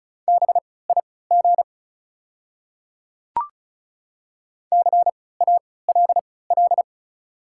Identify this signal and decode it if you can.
Morse code, heard as a keyed tone, about 36 wpm, spaced as for 20 wpm, 700 Hz. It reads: BIG CALL
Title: 3x Character Spacing - Top 300 Words - 35wpm